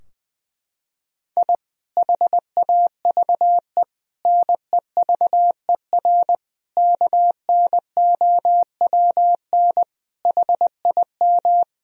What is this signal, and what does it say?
Morse code, heard as a keyed tone, about 20 wpm, 700 Hz. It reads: I HAVE NEVER KNOWN HIM
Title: Sentences from Top 1000 Words - Review Part 3 - 20wpm